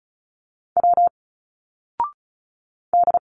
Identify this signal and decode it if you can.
Morse code, heard as a keyed tone, about 36 wpm, 700 Hz. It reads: W D
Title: Single Letters 35wpm